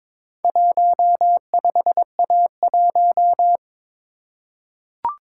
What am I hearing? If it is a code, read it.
Morse code, heard as a keyed tone, about 22 wpm, 690 Hz. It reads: 15A1
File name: practice / 22wpm / Four Letter-Number Combinations - 22wpm